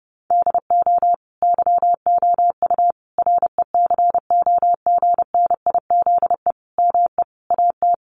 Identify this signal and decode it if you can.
Morse code, heard as a keyed tone, about 28 wpm, 700 Hz. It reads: DO YOU RECOGNIZE ME AT
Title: Sentences from Top 1400 Words 30wpm